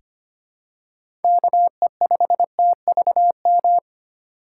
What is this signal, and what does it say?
Morse code, heard as a keyed tone, about 25 wpm, 700 Hz. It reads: KE5TVM
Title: USA Call Signs - 6 Characters - 6x Character Spacing - 25wpm